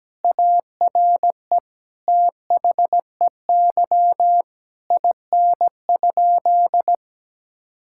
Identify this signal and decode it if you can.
Morse code, heard as a keyed tone, about 17 wpm, 700 Hz. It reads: ARE THEY IN?
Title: Sentences from Top 1000 Words - Review Part 5 - 17wpm